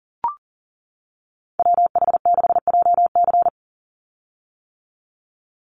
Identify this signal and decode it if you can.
Morse code, heard as a keyed tone, about 40 wpm, 700 Hz. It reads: WH6JC
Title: USA Call Signs 40wpm